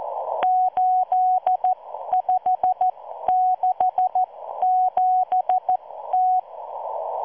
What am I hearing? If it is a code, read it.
Morse code, heard as a keyed tone, about 14 wpm, spaced as for 12 wpm, 740 Hz. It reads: 8567T